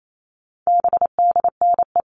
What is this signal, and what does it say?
Morse code, heard as a keyed tone, about 28 wpm, 700 Hz. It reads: BDNE
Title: All Cut Numbers - 4-Digit 28wpm